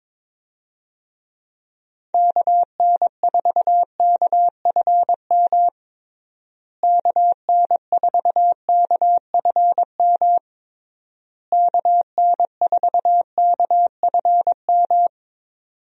Morse code, heard as a keyed tone, about 22 wpm, 700 Hz. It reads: KN4KFM KN4KFM KN4KFM
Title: USA Call Signs - 6 Characters - Repeated 3x - 22wpm